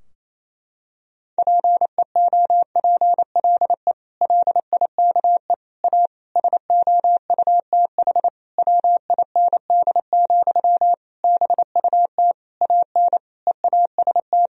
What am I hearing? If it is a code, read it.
Morse code, heard as a keyed tone, about 28 wpm, 700 Hz. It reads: PEOPLE LIKE A SOUTH WIND, BUT AN EAST
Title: Sentences from Top 1000 Words - Review Part 3 - 28wpm